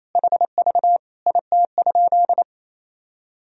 Morse code, heard as a keyed tone, about 28 wpm, 700 Hz. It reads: HV IT?
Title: Sentences from Top 100 Words and Abbreviations 28wpm